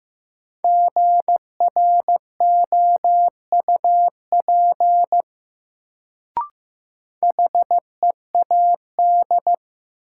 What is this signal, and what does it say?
Morse code, heard as a keyed tone, about 15 wpm, 700 Hz. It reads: GROUP HEAD
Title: Sets of 3 Words - New Word Drill - 15wpm